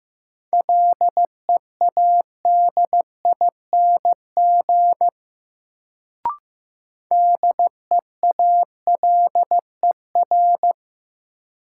Morse code, heard as a keyed tone, about 16 wpm, 700 Hz. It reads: LEADING DEALER
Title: Sentences from Top 1100 Words - New Word Drill - 15wpm